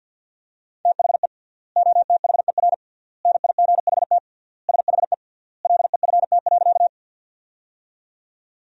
Morse code, heard as a keyed tone, about 51 wpm, 700 Hz. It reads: THE OTHER NIGHT SHE LEFT.
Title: 3x Word-Spacing - Sentences from Top 300 Words - 50wpm